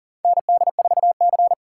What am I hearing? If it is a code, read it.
Morse code, heard as a keyed tone, about 41 wpm, 700 Hz. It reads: ND4C
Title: USA Call Signs - 4 Characters - 3x Character Spacing - 40wpm